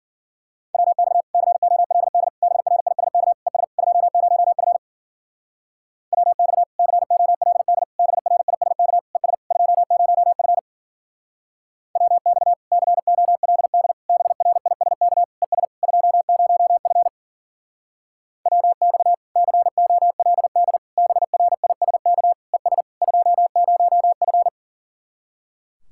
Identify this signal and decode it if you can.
Morse code, heard as a keyed tone, about 50 wpm, 700 Hz. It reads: WX COLD BRISK ES 20F WX COLD BRISK ES 20F WX COLD BRISK ES 20F WX COLD BRISK ES 20F
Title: Speed-Racing Common QSO Phrases Encore - 40wpm